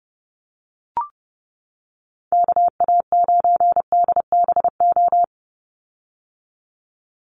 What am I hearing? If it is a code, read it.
Morse code, heard as a keyed tone, about 30 wpm, 700 Hz. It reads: KA9DBO
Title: USA Call Signs 30wpm